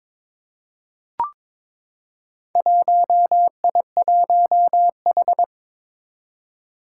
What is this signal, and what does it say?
Morse code, heard as a keyed tone, about 22 wpm, 700 Hz. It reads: 1I1H
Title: Four Challenging Letter-Number Combinations 22wpm